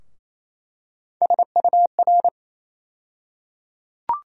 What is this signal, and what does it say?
Morse code, heard as a keyed tone, about 28 wpm, 700 Hz. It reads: SUR
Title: USA Call Sign Postfixes - 3 Letters 28wpm